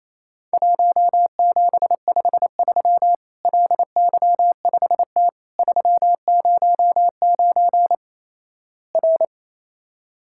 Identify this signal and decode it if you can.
Morse code, heard as a keyed tone, about 28 wpm, 700 Hz. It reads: 1753 LY5T 309 R
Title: Worked All Europe DX Contest - QTC Exchange - 28wpm